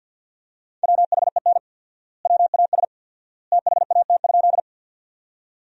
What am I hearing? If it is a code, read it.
Morse code, heard as a keyed tone, about 50 wpm, 700 Hz. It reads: WHEN WAS THAT?
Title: 4x Word-Spacing - Sentences from Top 100 Words - 50wpm